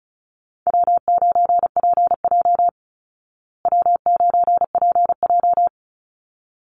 Morse code, heard as a keyed tone, about 34 wpm, 700 Hz. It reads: W9PJ W9PJ
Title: Call Signs Repeated 3x - 35wpm